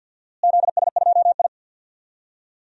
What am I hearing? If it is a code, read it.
Morse code, heard as a keyed tone, about 51 wpm, 700 Hz. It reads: ZS2I
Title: DXCC-100 Call Signs - 4 Characters - Repeated 2x - 50wpm